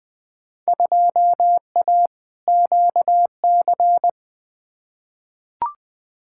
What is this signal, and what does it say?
Morse code, heard as a keyed tone, about 21 wpm, 700 Hz. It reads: 2A QC
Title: Field Day Exchanges - 20wpm